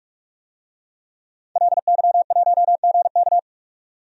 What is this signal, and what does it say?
Morse code, heard as a keyed tone, about 44 wpm, 690 Hz. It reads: LY1GK
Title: DXCC-100 Call Signs - 5 Characters - 45wpm